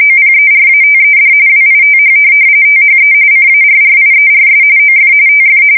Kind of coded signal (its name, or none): RTTY